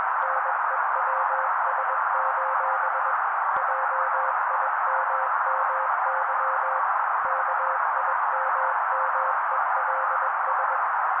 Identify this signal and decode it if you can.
Morse code, heard as a keyed tone, about 21 wpm, 530 Hz. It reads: NEWS: JIMMY KIMMELS
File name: daily / 20wpm noise